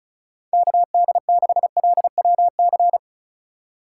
Morse code, heard as a keyed tone, about 43 wpm, 700 Hz. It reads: KD6LWC
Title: Speed Racing - USA 2x3 Call Signs 35wpm